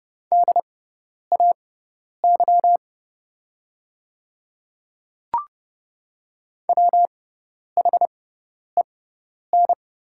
Morse code, heard as a keyed tone, about 30 wpm, spaced as for 10 wpm, 700 Hz. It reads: DAY WHEN
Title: 6x Character Spacing - Top 100 Words - 30wpm